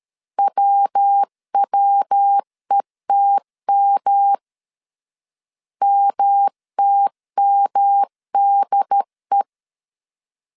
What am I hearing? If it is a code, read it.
Morse code, heard as a keyed tone, about 13 wpm, 800 Hz. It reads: WWETM MTMDE
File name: CW / DET / 1 3 1500C S A 12wpm 2905